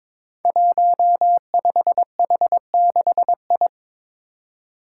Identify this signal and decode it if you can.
Morse code, heard as a keyed tone, about 22 wpm, 700 Hz. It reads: 15H6I
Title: Five Challenging Letter-Number Combinations 22wpm